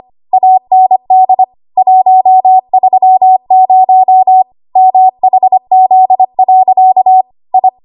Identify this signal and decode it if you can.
Morse code, heard as a keyed tone, about 25 wpm, 750 Hz. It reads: AND 130 MHZ. I
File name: Archive / 25 WPM / 25WPM